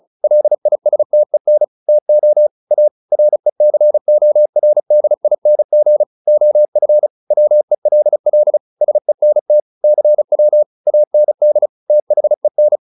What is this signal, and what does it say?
Morse code, heard as a keyed tone, about 34 wpm, 600 Hz. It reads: LISTEN TO A RECORDING OF WELL SENT CW AND THEN